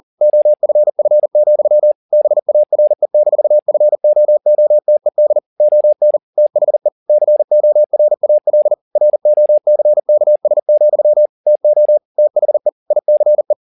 Morse code, heard as a keyed tone, about 40 wpm, 600 Hz. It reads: OFF, BARE-FOOTED ON THE CORAL ROCKS, TO THE ICE